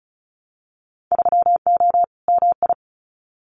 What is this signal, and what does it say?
Morse code, heard as a keyed tone, about 34 wpm, 700 Hz. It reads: 3O MI
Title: Winter Field Day Exchanges 35wpm